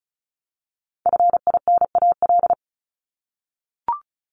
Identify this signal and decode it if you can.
Morse code, heard as a keyed tone, about 34 wpm, 700 Hz. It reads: FINAL
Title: Top 500 Words Repeated 35wpm